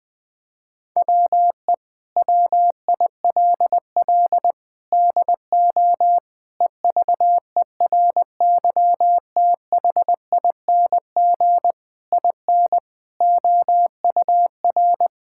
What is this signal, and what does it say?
Morse code, heard as a keyed tone, about 21 wpm, 700 Hz. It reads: WE WILL DO EVERYTHING IN OUR
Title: Sentences with 8-Word Phrases - 20wpm